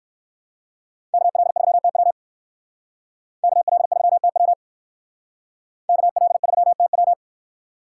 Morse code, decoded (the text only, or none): KL3TW KL3TW KL3TW